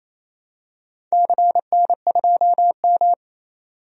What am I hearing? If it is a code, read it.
Morse code, heard as a keyed tone, about 28 wpm, 700 Hz. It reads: CN2M
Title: DXCC-100 Call Signs - 4 Characters - 28wpm